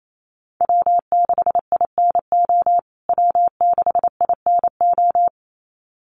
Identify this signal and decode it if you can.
Morse code, heard as a keyed tone, about 28 wpm, 700 Hz. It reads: W6INO W6INO
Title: USA Call Signs Repeated 2x 28wpm